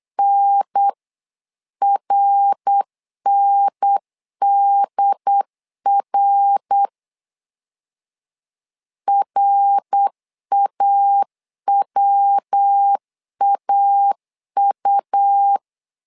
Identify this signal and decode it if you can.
Morse code, heard as a keyed tone, about 8 wpm, 800 Hz. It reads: NRNDR RAWAU
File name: LEZIONI / 3 MWRO / 3 MWR 8wpm 2444